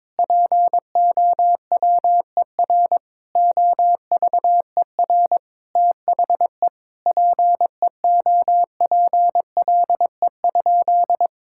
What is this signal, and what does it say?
Morse code, heard as a keyed tone, about 22 wpm, 700 Hz. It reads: POWER OVER THE PEOPLE?
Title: Sentences from Top 1000 Words - Review Part 5 - 22wpm